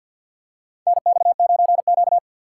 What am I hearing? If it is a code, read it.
Morse code, heard as a keyed tone, about 51 wpm, 700 Hz. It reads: NX9Q
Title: USA Call Signs - 4x Character Spacing - 50wpm